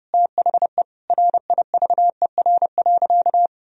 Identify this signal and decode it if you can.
Morse code, heard as a keyed tone, about 28 wpm, 700 Hz. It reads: THE RIVER.
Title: Sentences from Top 1000 Words - Review Part 4 - 30wpm